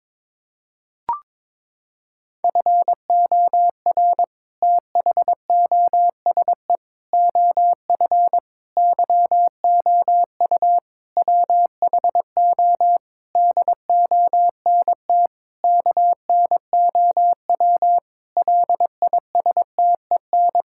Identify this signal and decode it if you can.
Morse code, heard as a keyed tone, about 22 wpm, 700 Hz. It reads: FOR THOSE OF YOU WHO DONT KNOW LISTEN
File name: Sentences with 7-Word Phrases - 22wpm